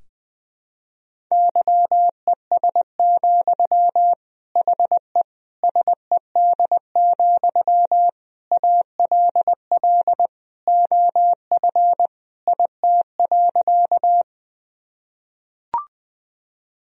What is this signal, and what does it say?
Morse code, heard as a keyed tone, about 21 wpm, 700 Hz. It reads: YES, HE SED, ALL OF IT.
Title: Sentences from Top 500 Words and Abbreviations 20wpm